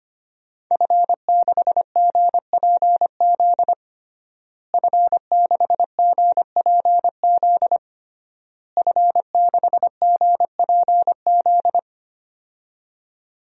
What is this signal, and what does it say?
Morse code, heard as a keyed tone, about 25 wpm, 700 Hz. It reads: F6GPZ F6GPZ F6GPZ
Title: DXCC-100 Call Signs - 5 Characters - Repeated 3x - 25wpm